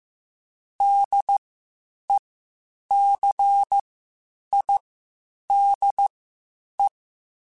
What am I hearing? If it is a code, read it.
Morse code, heard as a keyed tone, about 15 wpm, spaced as for 8 wpm, 800 Hz. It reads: DECIDE
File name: Spaced / Chuckles Spaced-mp3s / HalfFull sp @13